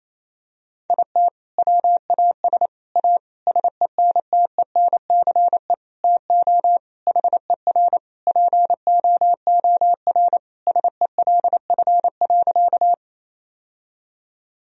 Morse code, heard as a keyed tone, about 28 wpm, 700 Hz. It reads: IT WAS A SENTENCE TO HER POOR SELF.
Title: Sentences from Top 1000 Words - Review Part 1 - 28wpm